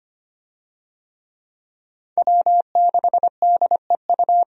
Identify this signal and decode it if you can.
Morse code, heard as a keyed tone, about 25 wpm, 700 Hz. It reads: W6DEU
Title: USA Call Signs - 5 Characters - Repeated 3x - 25wpm